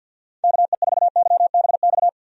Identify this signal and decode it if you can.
Morse code, heard as a keyed tone, about 52 wpm, 700 Hz. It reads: KE4YBX
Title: USA Call Signs - 6 Characters - 50wpm